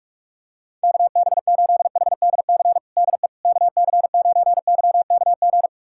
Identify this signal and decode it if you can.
Morse code, heard as a keyed tone, about 41 wpm, 700 Hz. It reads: KB8HDC DE KC9YKG
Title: USA Callsign DE Callsign - 6 Characters - 45wpm